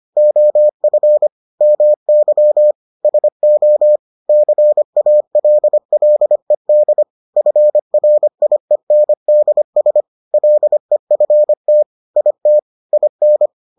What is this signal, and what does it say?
Morse code, heard as a keyed tone, about 25 wpm, 600 Hz. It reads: OF MY SO CALLED FRIENDS LEFT IT IN